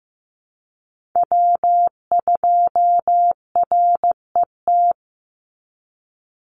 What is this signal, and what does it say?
Morse code, heard as a keyed tone, about 15 wpm, 700 Hz. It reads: W2RET
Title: USA Call Signs 15wpm